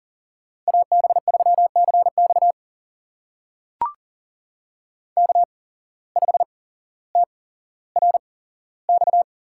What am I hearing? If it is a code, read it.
Morse code, heard as a keyed tone, about 40 wpm, spaced as for 17 wpm, 700 Hz. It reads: AB3CX K5TRX
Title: USA Call Signs - 8x Character Spacing - 40wpm